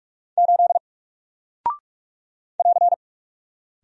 Morse code, heard as a keyed tone, about 44 wpm, 700 Hz. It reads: 8 <AR>
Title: Single Character (Letter:Number:Punctuation:Prosign) Alt 45wpm